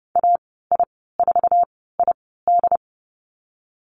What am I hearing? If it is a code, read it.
Morse code, heard as a keyed tone, about 30 wpm, spaced as for 17 wpm, 700 Hz. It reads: AI4ID
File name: Lesson 15g - Number 5 - Call Signs 30wpm